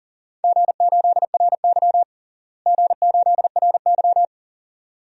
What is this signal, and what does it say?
Morse code, heard as a keyed tone, about 40 wpm, 700 Hz. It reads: G8RY G8RY
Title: DXCC-100 Call Signs - 4 Characters - Repeated 2x - 40wpm